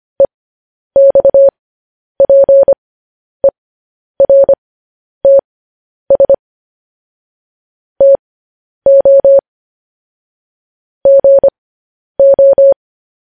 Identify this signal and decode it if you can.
Morse code, heard as a keyed tone, about 25 wpm, spaced as for 10 wpm, 550 Hz. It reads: EXPERTS TO GO
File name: podcast / News headlines-1x-25x10